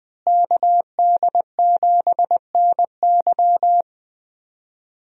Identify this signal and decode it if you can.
Morse code, heard as a keyed tone, about 20 wpm, 700 Hz. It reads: KD7NY
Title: USA Call Signs - 5 Characters - Repeated 3x - 20wpm